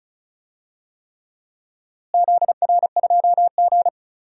Morse code, heard as a keyed tone, about 36 wpm, 700 Hz. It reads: ZR2G
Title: DXCC-100 Call Signs - 4 Characters - Repeated 3x - 35wpm